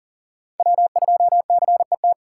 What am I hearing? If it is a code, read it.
Morse code, heard as a keyed tone, about 40 wpm, 700 Hz. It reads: W2CET
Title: USA Call Signs - 6x Character Spacing - 40wpm